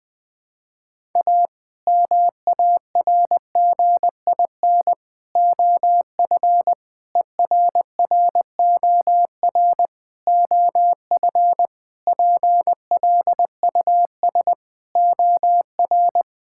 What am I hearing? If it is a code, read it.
Morse code, heard as a keyed tone, about 19 wpm, 700 Hz. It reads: A MARGIN OF ERROR OF PLUS OR
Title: Sets of 8 Words - 20wpm